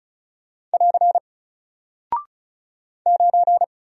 Morse code, heard as a keyed tone, about 36 wpm, 700 Hz. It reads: <AR> 9
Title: Single Character (Letter:Number:Punctuation:Prosign) 35wpm